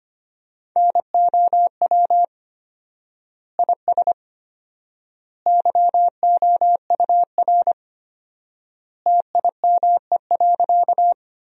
Morse code, heard as a keyed tone, about 25 wpm, 700 Hz. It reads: NOW IS YOUR TIME.
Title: 4x Word-Spacing - Sentences from Top 100 Words - 25wpm